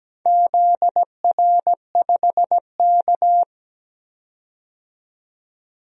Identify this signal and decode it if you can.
Morse code, heard as a keyed tone, about 17 wpm, 700 Hz. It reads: ZR5K
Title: DXCC-100 Call Signs - 4 Characters - 17wpm